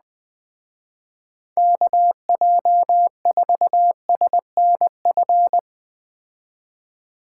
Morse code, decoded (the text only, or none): KJ4SNF